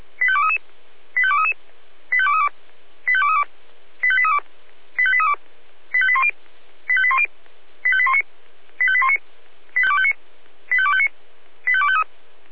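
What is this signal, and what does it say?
Five-tone selective call (ZVEI), heard as quick runs of five tones, tones of about 70 ms. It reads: 85233 85233 85232 85232 85832 85832 85810 85810 85819 85819 85259 85253
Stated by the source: ZVEI five-tone call 85264 received by SDR.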